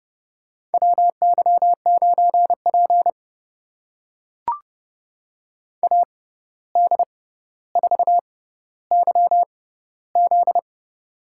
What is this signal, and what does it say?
Morse code, heard as a keyed tone, about 30 wpm, spaced as for 14 wpm, 700 Hz. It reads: WY9P AD4YZ
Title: USA Call Signs - 6x Character Spacing - 30wpm